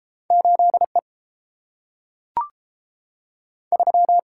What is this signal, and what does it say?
Morse code, heard as a keyed tone, about 30 wpm, 700 Hz. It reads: 8 E 3